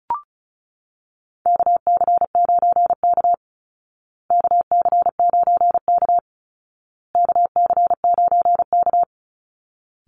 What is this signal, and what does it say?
Morse code, heard as a keyed tone, about 34 wpm, 700 Hz. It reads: KC9K KC9K KC9K
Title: Call Signs Repeated 2x - 35wpm